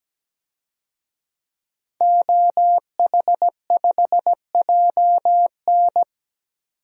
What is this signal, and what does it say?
Morse code, heard as a keyed tone, about 17 wpm, 700 Hz. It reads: OH5JN
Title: DXCC-100 Call Signs - 5 Characters - 17wpm